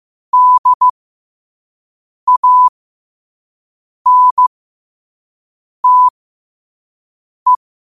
Morse code, heard as a keyed tone, about 15 wpm, spaced as for 5 wpm, 1000 Hz. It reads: DANTE
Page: Quotes for Fri, 15 Aug 2025 in Morse Code at 5 words per minute.